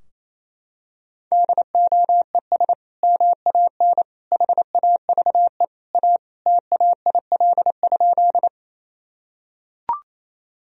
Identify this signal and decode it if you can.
Morse code, heard as a keyed tone, about 27 wpm, 700 Hz. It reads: DOES MAN HAVE A TAIL?
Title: Sentences from Top 1000 Words - Review Part 3 - 28wpm